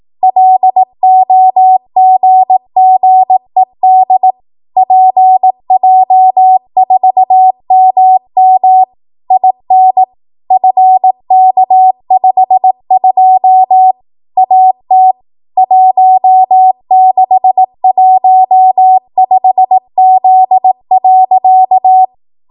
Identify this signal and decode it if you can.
Morse code, heard as a keyed tone, about 18 wpm, 750 Hz. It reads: LOGGED PJ4MM IN FK52 AT 1615Z.
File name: Archive / 18 WPM / 18WPM